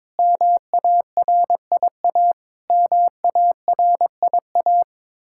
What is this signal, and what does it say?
Morse code, heard as a keyed tone, about 22 wpm, 700 Hz. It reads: MARIA MARIA
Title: Practice QSO - Exchange 1 - Beginning - 22wpm